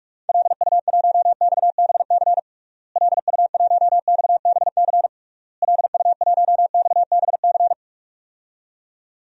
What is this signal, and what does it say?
Morse code, heard as a keyed tone, about 46 wpm, 700 Hz. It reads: LU1XBC LU1XBC LU1XBC
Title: DXCC-100 Call Signs - 6 Characters - Repeated 3x - 45wpm